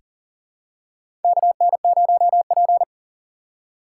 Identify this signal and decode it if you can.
Morse code, heard as a keyed tone, about 40 wpm, 700 Hz. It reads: KN0P
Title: USA Call Signs - 4 Characters - 3x Character Spacing - 40wpm